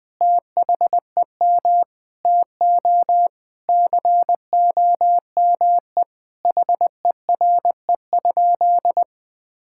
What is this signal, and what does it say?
Morse code, heard as a keyed tone, about 21 wpm, 700 Hz. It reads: THEM TO COME HERE?
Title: Sentences from Top 1000 Words - Review Part 5 - 20wpm